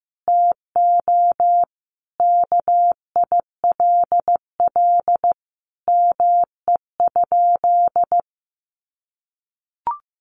Morse code, heard as a keyed tone, about 15 wpm, 700 Hz. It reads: TO KILL ME?
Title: Sentences from Top 1400 Words 15wpm